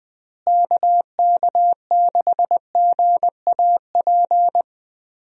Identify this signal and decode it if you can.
Morse code, heard as a keyed tone, about 20 wpm, 700 Hz. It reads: KK6GAP
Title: USA Call Signs - 6 Characters - Repeated 3x - 20wpm